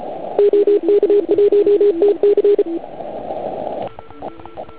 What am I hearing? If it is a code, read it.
Morse code, heard as a keyed tone, about 33 wpm, 410 Hz. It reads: OK1TC
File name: Archiv / CUC149 / OK1IF CUC149